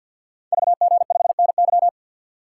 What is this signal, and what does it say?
Morse code, heard as a keyed tone, about 52 wpm, 700 Hz. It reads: VG5NY
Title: Code Groups - 5 Letter:Numbers - 50wpm